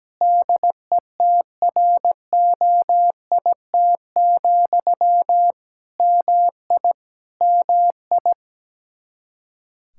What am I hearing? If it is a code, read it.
Morse code, heard as a keyed tone, about 17 wpm, 700 Hz. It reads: DETROIT, MI MI
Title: Practice QSO - Exchange 1 - Beginning - 17wpm